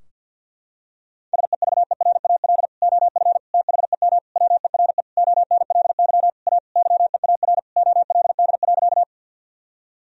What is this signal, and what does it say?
Morse code, heard as a keyed tone, about 50 wpm, 700 Hz. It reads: SEVERAL OF THEM WERE ONLY A YEAR OLD.
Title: Sentences from Top 1000 Words - Review Part 2 - 50wpm